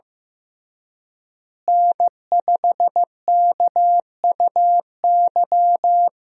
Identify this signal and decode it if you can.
Morse code, heard as a keyed tone, about 15 wpm, 700 Hz. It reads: N5KUY
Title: USA Call Signs - 5 Characters - Repeated 3x - 15wpm